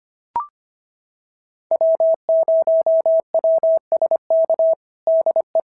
Morse code, heard as a keyed tone, about 25 wpm, 690 Hz. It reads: W0WSK DE
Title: SOTA QSO Practice 25wpm